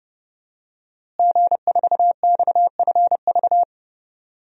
Morse code, heard as a keyed tone, about 32 wpm, 700 Hz. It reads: G4XFV
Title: DXCC-100 Call Signs - 5 Characters - Repeated 3x - 30wpm